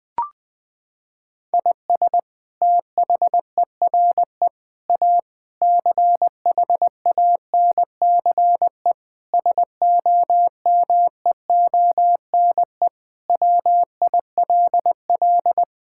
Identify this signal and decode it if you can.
Morse code, heard as a keyed tone, about 21 wpm, 700 Hz. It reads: IS THERE A CHANCE SOMEONE WILL
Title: Sentences from Top 1000 Words - Review Part 6 - 20wpm